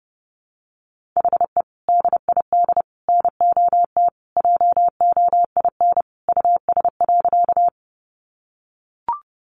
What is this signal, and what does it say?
Morse code, heard as a keyed tone, about 30 wpm, 700 Hz. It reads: HE DID NOT JOIN US.
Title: Sentences from Top 1400 Words 30wpm